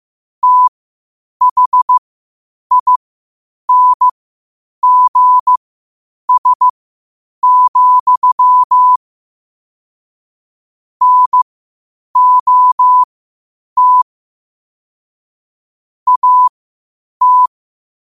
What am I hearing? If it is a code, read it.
Morse code, heard as a keyed tone, about 15 wpm, spaced as for 8 wpm, 1000 Hz. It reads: THINGS, NOT AT
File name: qotdfiles / 8wpm